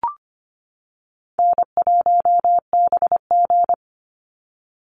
Morse code, heard as a keyed tone, about 25 wpm, 700 Hz. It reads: N1BG
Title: Easy Call Signs - 25wpm